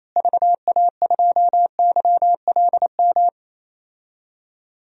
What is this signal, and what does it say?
Morse code, heard as a keyed tone, about 28 wpm, 700 Hz. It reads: VA2YLM
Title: DXCC-100 Call Signs - 6 Characters - Repeated 2x - 28wpm